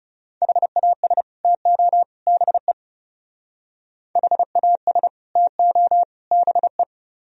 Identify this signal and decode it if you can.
Morse code, heard as a keyed tone, about 37 wpm, 700 Hz. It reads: HAS TO BE HAS TO BE
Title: Speed-Racing - Sets of 3 Words 30wpm